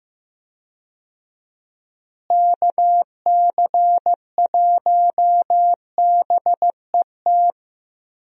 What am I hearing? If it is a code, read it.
Morse code, heard as a keyed tone, about 15 wpm, 700 Hz. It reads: KC1BET
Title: USA Call Signs - 6 Characters - Repeated 2x - 15wpm